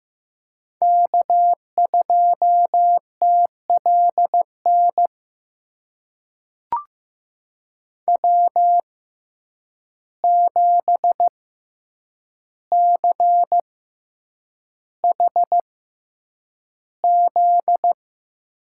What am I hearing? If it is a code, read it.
Morse code, heard as a keyed tone, about 15 wpm, spaced as for 8 wpm, 700 Hz. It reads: K2TLN W7CHZ
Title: USA Call Signs - 5 Characters - 6x Character Spacing - 15wpm